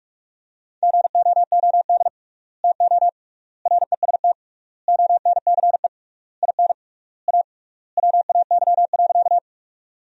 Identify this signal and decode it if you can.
Morse code, heard as a keyed tone, about 46 wpm, 700 Hz. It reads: GOOD TO REST ONCE IN A WAY.
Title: 3x Word-Spacing - Sentences from Top 500 Words - 45wpm